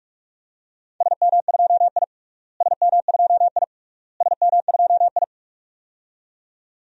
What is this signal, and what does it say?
Morse code, heard as a keyed tone, about 45 wpm, 700 Hz. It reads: SM2I SM2I SM2I
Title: DXCC-100 Call Signs - 4 Characters - Repeated 3x - 45wpm